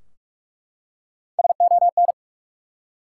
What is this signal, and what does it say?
Morse code, heard as a keyed tone, about 44 wpm, 700 Hz. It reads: SON
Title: Three Letter Words - 45wpm